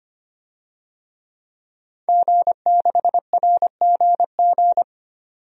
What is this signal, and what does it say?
Morse code, heard as a keyed tone, about 25 wpm, 700 Hz. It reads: G6RGG
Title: DXCC-100 Call Signs - 5 Characters - 25wpm